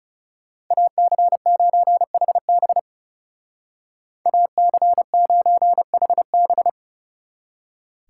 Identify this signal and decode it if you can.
Morse code, heard as a keyed tone, about 36 wpm, 700 Hz. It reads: AC9HB AC9HB
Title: Speed Racing - USA 2x2 Call Signs 30wpm